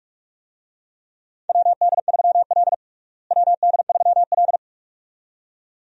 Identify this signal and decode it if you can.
Morse code, heard as a keyed tone, about 46 wpm, 700 Hz. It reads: WD3L WD3L
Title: USA Call Signs - 4 Characters - Repeated 2x - 45wpm